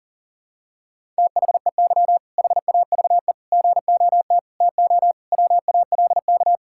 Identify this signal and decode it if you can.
Morse code, heard as a keyed tone, about 40 wpm, 700 Hz. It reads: THEY HAVE GOT TO WALK
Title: Sentences from Top 1000 Words - Review Part 6 - 40wpm